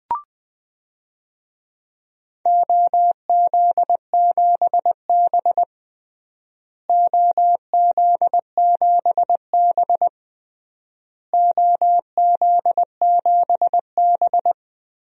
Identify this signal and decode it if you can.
Morse code, heard as a keyed tone, about 21 wpm, 700 Hz. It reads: OZ7B OZ7B OZ7B
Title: DXCC-100 Call Signs - 4 Characters - Repeated 3x - 20wpm